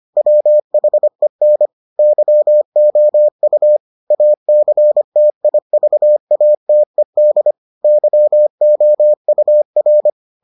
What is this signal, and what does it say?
Morse code, heard as a keyed tone, about 25 wpm, 600 Hz. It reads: WHEN YOU ACTIVATED YOUR